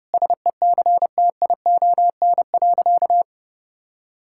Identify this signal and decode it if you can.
Morse code, heard as a keyed tone, about 30 wpm, 700 Hz. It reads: SECTION.
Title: Sentences from Top 1000 Words - Review Part 5 - 30wpm